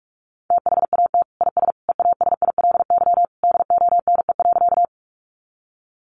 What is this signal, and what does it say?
Morse code, heard as a keyed tone, about 44 wpm, 700 Hz. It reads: THAT IS EASILY DONE.
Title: Sentences from Top 900 Words 45wpm